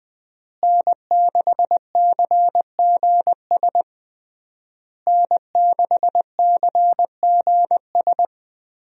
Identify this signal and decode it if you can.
Morse code, heard as a keyed tone, about 21 wpm, 700 Hz. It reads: N6CGS N6CGS
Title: USA Call Signs - 5 Characters - Repeated 3x - 20wpm